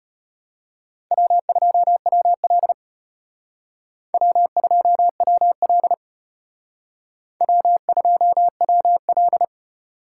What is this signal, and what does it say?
Morse code, heard as a keyed tone, about 32 wpm, 700 Hz. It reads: W2WL W2WL W2WL